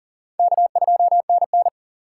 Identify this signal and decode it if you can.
Morse code, heard as a keyed tone, about 40 wpm, 700 Hz. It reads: K2NN
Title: USA Call Signs - 4 Characters - 2x Character Spacing - 40wpm